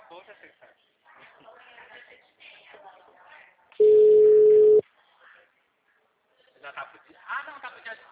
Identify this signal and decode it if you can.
Ringback tone (Europe, 425 Hz), with speech.